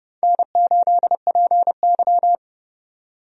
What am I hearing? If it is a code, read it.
Morse code, heard as a keyed tone, about 30 wpm, 700 Hz. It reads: N8PY
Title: USA Call Signs - 4 Characters - 30wpm